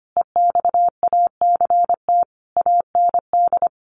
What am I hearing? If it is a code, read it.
Morse code, heard as a keyed tone, about 25 wpm, 700 Hz. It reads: EXACT AND
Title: Sentences from Top 1300 Words 25wpm